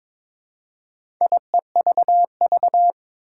Morse code, heard as a keyed tone, about 22 wpm, 700 Hz. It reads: IEVV
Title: Four Challenging Letter-Number Combinations 22wpm